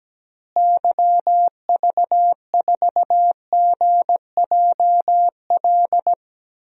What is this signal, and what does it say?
Morse code, heard as a keyed tone, about 17 wpm, 700 Hz. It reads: YV4GJL
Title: DXCC-100 Call Signs - 6 Characters - Repeated 2x - 17wpm